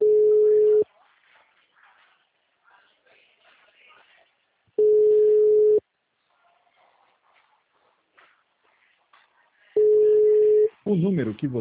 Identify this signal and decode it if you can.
Ringback tone (Europe, 425 Hz), with speech.